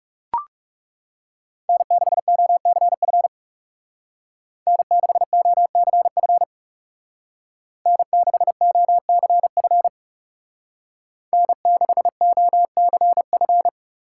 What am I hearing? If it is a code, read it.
Morse code, heard as a keyed tone, about 37 wpm, 700 Hz. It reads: N6OCF N6OCF N6OCF N6OCF
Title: Speed Racing - USA 1x3 Call Signs 30wpm